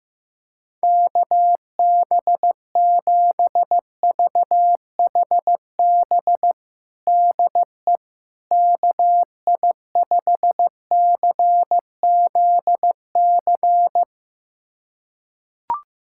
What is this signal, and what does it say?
Morse code, heard as a keyed tone, about 15 wpm, 700 Hz. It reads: KB7VHB DE KI5CZC
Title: USA Callsign DE Callsign - 6 Characters - 15wpm